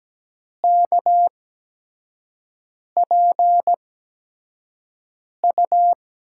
Morse code, heard as a keyed tone, about 17 wpm, spaced as for 5 wpm, 700 Hz. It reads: KPU
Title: USA Call Signs - 5 Characters - 8x Character Spacing - 17wpm